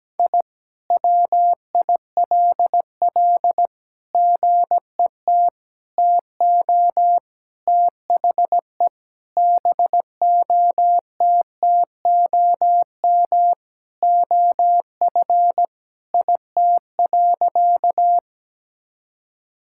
Morse code, heard as a keyed tone, about 17 wpm, 700 Hz. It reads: I WILL GET TO THE BOTTOM OF IT.
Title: Sentences from Top 1000 Words - Review Part 6 - 17wpm